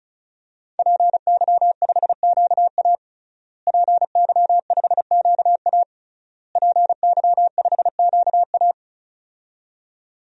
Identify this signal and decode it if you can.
Morse code, heard as a keyed tone, about 36 wpm, 700 Hz. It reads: PY5QA PY5QA PY5QA
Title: DXCC-100 Call Signs - 5 Characters - Repeated 3x - 35wpm